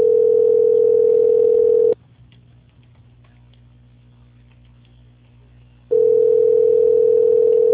Ringback tone (North America), 440 + 480 Hz.
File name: out- -2037- - - .24756